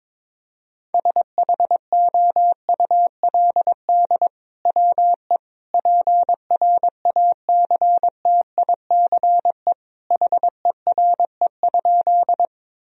Morse code, heard as a keyed tone, about 21 wpm, 700 Hz. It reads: SHOULD WE PRACTICE HERE?
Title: Sentences from Top 1000 Words - Review Part 4 - 22wpm